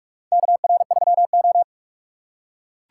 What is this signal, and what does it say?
Morse code, heard as a keyed tone, about 46 wpm, 700 Hz. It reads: KR3O
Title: USA Call Signs - 4 Characters - 2x Character Spacing - 45wpm